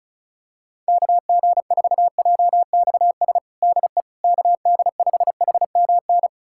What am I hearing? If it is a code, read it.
Morse code, heard as a keyed tone, about 34 wpm, 700 Hz. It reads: KG4JXS DE KD5HMN
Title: USA Callsign DE Callsign - 6 Characters - 35wpm